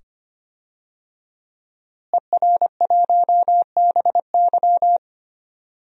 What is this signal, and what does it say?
Morse code, heard as a keyed tone, about 25 wpm, 700 Hz. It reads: ER1BY
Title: DXCC-100 Call Signs - 5 Characters - Repeated 2x - 25wpm